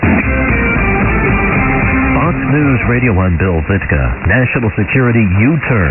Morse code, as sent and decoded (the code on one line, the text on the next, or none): none